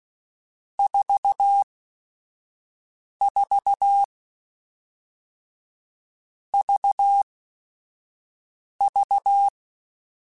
Morse code, heard as a keyed tone, about 16 wpm, 800 Hz. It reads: 4 4 V V
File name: Groups@5 / Groups@5 mp3s / VB46@5